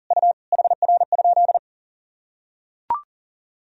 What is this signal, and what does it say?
Morse code, heard as a keyed tone, about 40 wpm, 700 Hz. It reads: U HR?
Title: Sentences from Top 500 Words and Abbreviations 40wpm